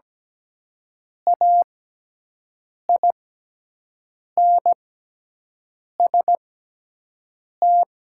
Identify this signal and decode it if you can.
Morse code, heard as a keyed tone, about 17 wpm, spaced as for 6 wpm, 700 Hz. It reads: AINST
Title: 6x Character Spacing - Top 500 Words - 17wpm